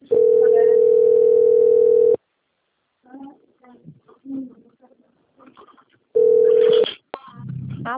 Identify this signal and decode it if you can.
Ringback tone (North America), 440 + 480 Hz, with speech.
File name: out- - - - - .3125